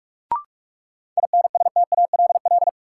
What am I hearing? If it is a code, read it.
Morse code, heard as a keyed tone, about 45 wpm, 700 Hz. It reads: INSTALL
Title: Sentences from Top 1600 Words - New Word Drill Alt - 45wpm